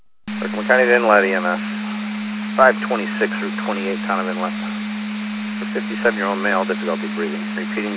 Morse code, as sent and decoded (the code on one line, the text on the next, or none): none